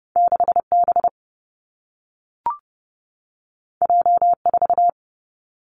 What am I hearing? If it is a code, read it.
Morse code, heard as a keyed tone, about 30 wpm, 700 Hz. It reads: 6 B J 4
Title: Two Challenging Letter-Number Combinations 30wpm